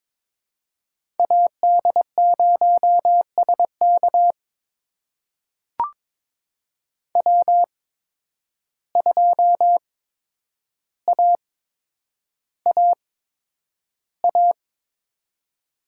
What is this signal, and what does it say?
Morse code, heard as a keyed tone, about 21 wpm, spaced as for 9 wpm, 700 Hz. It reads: AD0SK W2AAA
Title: USA Call Signs - 5 Characters - 8x Character Spacing - 22wpm